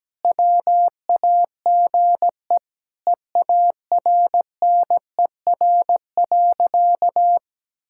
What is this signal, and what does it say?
Morse code, heard as a keyed tone, about 17 wpm, 700 Hz. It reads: WAGE EARNER.
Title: Pangram Challenge v2 - 17wpm